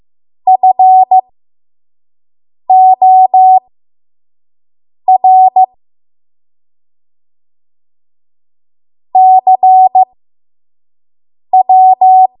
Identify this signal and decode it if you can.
Morse code, heard as a keyed tone, about 15 wpm, spaced as for 5 wpm, 750 Hz. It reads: FOR CW
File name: Morse / 05WPM